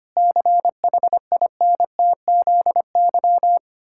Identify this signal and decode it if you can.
Morse code, heard as a keyed tone, about 25 wpm, 700 Hz. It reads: CHINTZY
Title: Pangram Challenge v2 - 25wpm